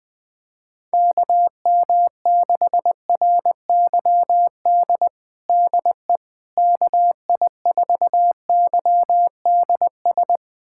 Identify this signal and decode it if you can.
Morse code, heard as a keyed tone, about 20 wpm, 700 Hz. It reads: KM6RYD DE KI4YDS
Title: USA Callsign DE Callsign - 6 Characters - 20wpm